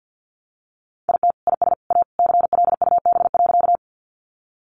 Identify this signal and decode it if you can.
Morse code, heard as a keyed tone, about 50 wpm, 700 Hz. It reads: IT IS A CLUB.
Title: Sentences from Top 1200 Words 50wpm